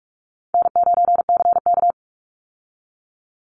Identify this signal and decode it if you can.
Morse code, heard as a keyed tone, about 44 wpm, 700 Hz. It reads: N9CK
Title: Easy Call Signs Repeated 3x - 45wpm